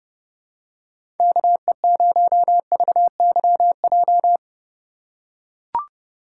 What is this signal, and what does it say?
Morse code, heard as a keyed tone, about 30 wpm, 700 Hz. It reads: KE0VYJ
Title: USA Call Signs - 6 Characters - 3x Character Spacing - 30wpm